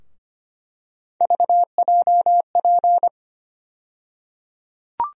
Morse code, heard as a keyed tone, about 25 wpm, 700 Hz. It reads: VJP
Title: USA Call Sign Postfixes - 3 Letters 25wpm